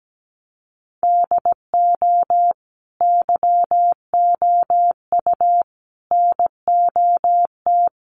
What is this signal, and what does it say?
Morse code, heard as a keyed tone, about 17 wpm, 700 Hz. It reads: DO YOU NOT
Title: Sentences from Top 600 Words 17wpm